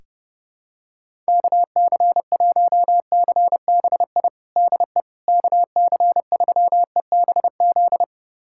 Morse code, heard as a keyed tone, about 30 wpm, 700 Hz. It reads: KC1CBI DE KC3EBZ
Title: USA Callsign DE Callsign - 6 Characters - 30wpm